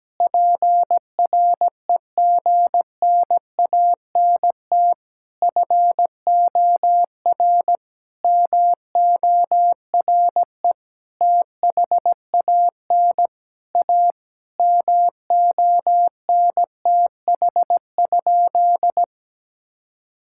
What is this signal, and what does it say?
Morse code, heard as a keyed tone, about 17 wpm, 700 Hz. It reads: PREGNANT FOR MORE THAN A MONTH?
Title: Sentences from Top 1000 Words - Review Part 6 - 17wpm